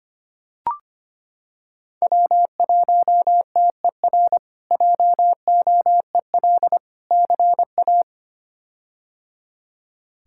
Morse code, heard as a keyed tone, about 25 wpm, 700 Hz. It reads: W1TER JOEL CA
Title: K1USN SST QSO Practice - 25wpm